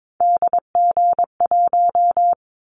Morse code, heard as a keyed tone, about 22 wpm, 700 Hz. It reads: DG1
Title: Callsign Prefixes for Top 25 DXCC - 22wpm